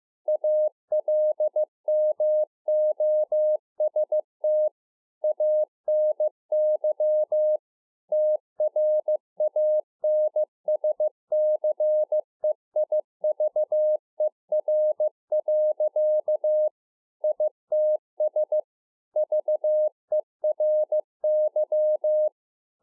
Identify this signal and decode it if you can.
Morse code, heard as a keyed tone, about 15 wpm, 610 Hz. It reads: ALMOST ANY TRANSCEIVER. ITS VERY
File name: 2011-04-mp3 / A SMALL TRANCEIVER 2 15WPM